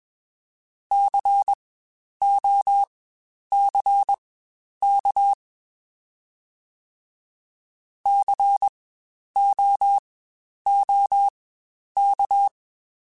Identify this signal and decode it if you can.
Morse code, heard as a keyed tone, about 21 wpm, spaced as for 10 wpm, 800 Hz. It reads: COCK COOK
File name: Spaced Alphas / Spaced Alphas mp3s / Cs sp @20